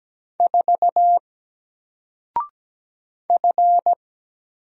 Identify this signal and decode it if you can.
Morse code, heard as a keyed tone, about 17 wpm, 700 Hz. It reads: 4 F
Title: Single Letter-Number Alt 17wpm